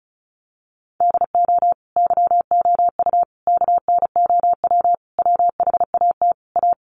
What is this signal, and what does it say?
Morse code, heard as a keyed tone, about 34 wpm, 700 Hz. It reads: DO YOU KNOW WHAT A
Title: Sentences from Top 1500 Words 35wpm